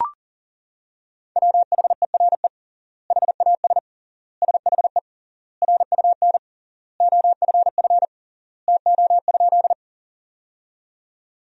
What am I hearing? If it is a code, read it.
Morse code, heard as a keyed tone, about 37 wpm, 700 Hz. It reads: WHERE HAS SHE RUN OFF TO?
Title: 3x Word-Spacing - Sentences from Top 300 Words - 40wpm